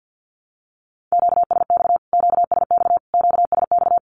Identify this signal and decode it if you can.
Morse code, heard as a keyed tone, about 50 wpm, 700 Hz. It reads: QSX QSX QSX
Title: Q-Codes Repeated 3x 50wpm